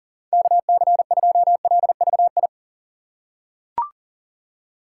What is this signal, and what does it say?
Morse code, heard as a keyed tone, about 40 wpm, 700 Hz. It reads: KC2LVI
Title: USA Call Signs - 6 Characters - 6x Character Spacing - 40wpm